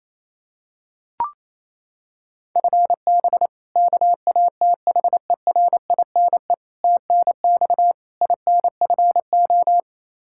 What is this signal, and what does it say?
Morse code, heard as a keyed tone, about 28 wpm, 700 Hz. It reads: FB KATHERINE TNX INFO
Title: K1USN SST QSO Practice 28wpm